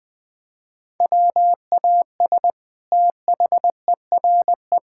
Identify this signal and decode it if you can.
Morse code, heard as a keyed tone, about 20 wpm, 700 Hz. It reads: WAS THERE
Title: Sentences from Top 1600 Words 20wpm